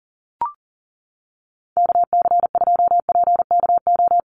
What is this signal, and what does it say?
Morse code, heard as a keyed tone, about 39 wpm, 700 Hz. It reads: KC2PKO
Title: USA Call Signs 40wpm